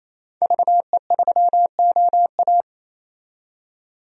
Morse code, heard as a keyed tone, about 27 wpm, 700 Hz. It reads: VE3OA
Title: DXCC-100 Call Signs - 5 Characters - 28wpm